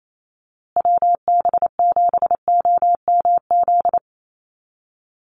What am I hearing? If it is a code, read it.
Morse code, heard as a keyed tone, about 28 wpm, 700 Hz. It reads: WB7OMZ
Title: USA Call Signs Repeated 2x 28wpm